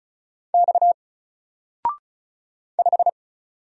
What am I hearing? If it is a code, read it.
Morse code, heard as a keyed tone, about 36 wpm, 700 Hz. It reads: X 5
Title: Single Character (Letter:Number:Punctuation:Prosign) 35wpm